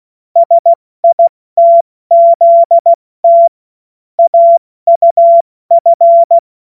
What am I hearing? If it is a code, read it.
Morse code, heard as a keyed tone, about 16 wpm, 680 Hz. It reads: SITZT AUF
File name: Morsekurs / Witze / Witz 09. Februar 2021, 70 F80